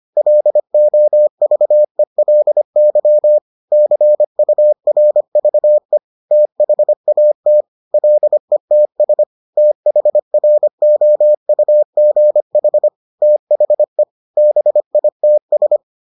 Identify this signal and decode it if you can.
Morse code, heard as a keyed tone, about 25 wpm, 600 Hz. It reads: LOVELY CURVE THAT LETS THROUGH THE BITS